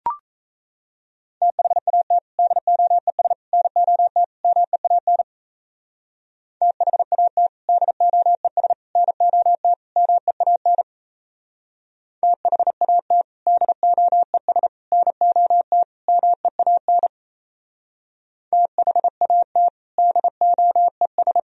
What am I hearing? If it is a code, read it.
Morse code, heard as a keyed tone, about 33 wpm, 700 Hz. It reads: THAT DOES NOT MEAN THAT DOES NOT MEAN THAT DOES NOT MEAN THAT DOES
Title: Speed-Racing - Sets of 4 Words 28wpm